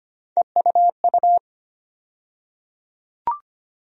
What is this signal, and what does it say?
Morse code, heard as a keyed tone, about 25 wpm, 700 Hz. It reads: EUU
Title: USA Call Sign Postfixes - 3 Letters 25wpm